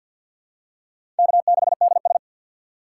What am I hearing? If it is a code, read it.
Morse code, heard as a keyed tone, about 52 wpm, 700 Hz. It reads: K6DS
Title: USA Call Signs - 4 Characters - 6x Character Spacing - 50wpm